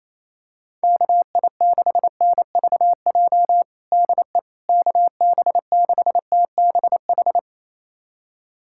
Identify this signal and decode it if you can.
Morse code, heard as a keyed tone, about 27 wpm, 700 Hz. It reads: KI6NVJ DE KB6TBH
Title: USA Callsign DE Callsign - 6 Characters - 28wpm